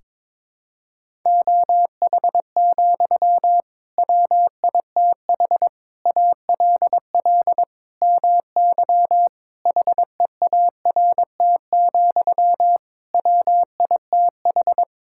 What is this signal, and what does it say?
Morse code, heard as a keyed tone, about 22 wpm, 700 Hz. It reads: OH, WITH ALL MY HEART, WITH
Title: Sentences from Top 1600 Words 22wpm